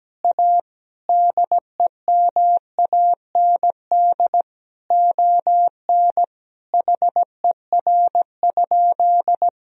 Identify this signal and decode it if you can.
Morse code, heard as a keyed tone, about 17 wpm, 700 Hz. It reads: A DEMAND ON HER?
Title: Sentences from Top 1000 Words - Review Part 4 - 17wpm